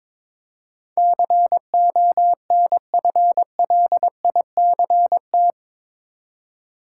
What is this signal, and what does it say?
Morse code, heard as a keyed tone, about 22 wpm, 700 Hz. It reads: CONFLICT
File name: practice / 22wpm / Eight Letter Words - 22wpm